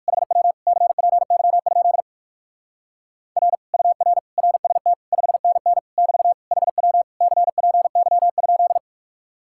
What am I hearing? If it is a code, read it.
Morse code, heard as a keyed tone, about 57 wpm, 700 Hz. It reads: HW CPY? R UR RST 5NN <BT> HW CPY?